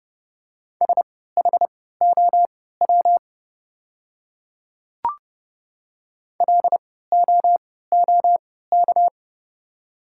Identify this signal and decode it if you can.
Morse code, heard as a keyed tone, about 29 wpm, spaced as for 17 wpm, 690 Hz. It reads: SHOW LOOK
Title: 3x Character Spacing - Top 300 Words - 30wpm